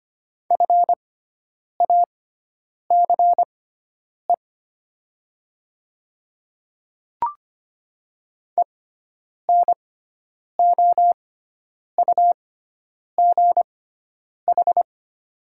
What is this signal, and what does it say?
Morse code, heard as a keyed tone, about 25 wpm, spaced as for 9 wpm, 700 Hz. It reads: FACE ENOUGH
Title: 6x Character Spacing - Top 500 Words - 25wpm